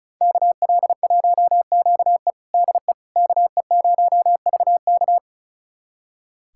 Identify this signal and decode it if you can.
Morse code, heard as a keyed tone, about 36 wpm, 690 Hz. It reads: KL1QE DE KE0VK
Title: USA Callsign DE Callsign - 5 Characters - 35wpm